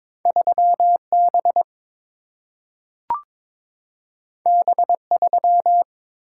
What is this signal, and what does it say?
Morse code, heard as a keyed tone, about 22 wpm, 700 Hz. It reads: 3 B B 3
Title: Single-Letter Single-Number 22wpm